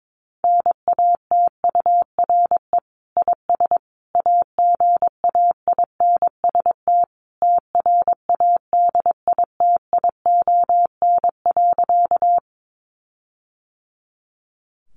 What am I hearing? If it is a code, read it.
Morse code, heard as a keyed tone, about 22 wpm, 700 Hz. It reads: NATURE IS AGAINST TRADITION.
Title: Sentences from Top 1500 Words 22wpm